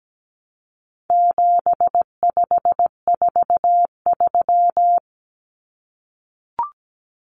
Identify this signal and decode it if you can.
Morse code, heard as a keyed tone, about 17 wpm, 700 Hz. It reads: 7543
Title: Common Cut Numbers - 4-Digit 17wpm